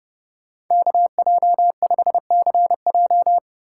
Morse code, heard as a keyed tone, about 36 wpm, 700 Hz. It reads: KJ5CJ
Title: Speed Racing - USA Call Signs 30wpm